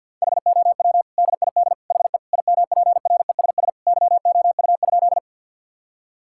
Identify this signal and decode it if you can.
Morse code, heard as a keyed tone, about 50 wpm, 700 Hz. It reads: HOW DID HE IMPRESS YOU?
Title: Sentences from Top 1600 Words 50wpm